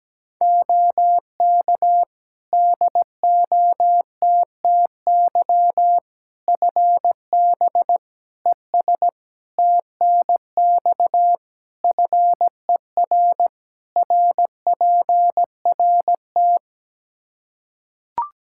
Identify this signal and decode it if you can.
Morse code, heard as a keyed tone, about 17 wpm, 700 Hz. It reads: OK DOTTY FB ES TNX FER RPRT
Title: K1USN SST QSO Practice 17wpm